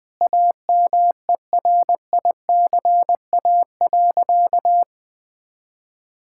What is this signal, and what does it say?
Morse code, heard as a keyed tone, about 21 wpm, 700 Hz. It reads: AMERICA.
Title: Sentences with 5-Word Phrases - 20wpm